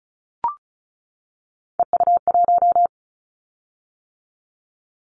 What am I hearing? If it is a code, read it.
Morse code, heard as a keyed tone, about 34 wpm, 700 Hz. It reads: EU1
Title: Callsign Prefixes for Top 100 DXCC - 35wpm